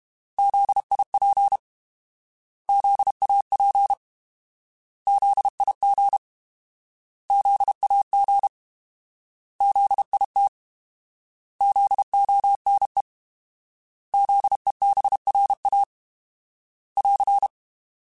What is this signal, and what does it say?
Morse code, heard as a keyed tone, about 32 wpm, 800 Hz. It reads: ZIP ZAP ZIG ZAG ZIT ZONE ZEBRA <AR>
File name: Alphabetical / Alphabetical mp3s / Zs @30